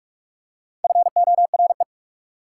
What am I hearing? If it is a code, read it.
Morse code, heard as a keyed tone, about 44 wpm, 690 Hz. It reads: FORE
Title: Prefixes and Suffixes 45wpm